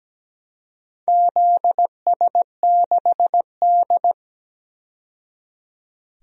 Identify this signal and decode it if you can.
Morse code, heard as a keyed tone, about 17 wpm, 700 Hz. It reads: ZS6D
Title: DXCC-100 Call Signs - 4 Characters - 17wpm